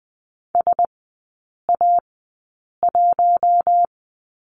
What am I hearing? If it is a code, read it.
Morse code, heard as a keyed tone, about 20 wpm, spaced as for 8 wpm, 700 Hz. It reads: SA1
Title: 047 Lesson 9e - Number 4 - Three Characters 20wpm